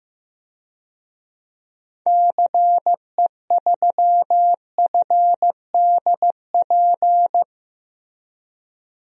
Morse code, heard as a keyed tone, about 15 wpm, 700 Hz. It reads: CE3FDP
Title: DXCC-100 Call Signs - 6 Characters - Repeated 2x - 15wpm